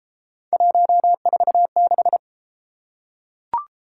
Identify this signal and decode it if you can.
Morse code, heard as a keyed tone, about 32 wpm, 700 Hz. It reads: 146